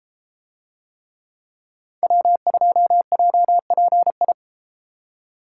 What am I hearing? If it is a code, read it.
Morse code, heard as a keyed tone, about 33 wpm, 700 Hz. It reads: W2JPI